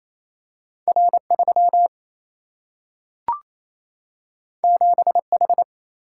Morse code, heard as a keyed tone, about 28 wpm, 700 Hz. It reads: R3 7H
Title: Single-Letter Single-Number 28wpm